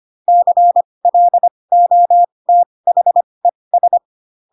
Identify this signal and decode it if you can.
Morse code, heard as a keyed tone, about 25 wpm, spaced as for 20 wpm, 700 Hz. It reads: CLOTHES
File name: Pinocchio / Pinocchio ch8 15 25 ik1hge